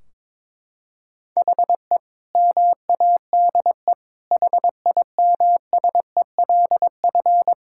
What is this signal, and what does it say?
Morse code, heard as a keyed tone, about 21 wpm, 700 Hz. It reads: HE MADE HIMSELF
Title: Sentences from Top 1600 Words 22wpm